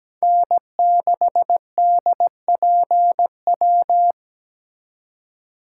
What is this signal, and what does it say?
Morse code, heard as a keyed tone, about 20 wpm, 700 Hz. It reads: N6DPW
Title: Speed Racing - USA 1x3 Call Signs 17wpm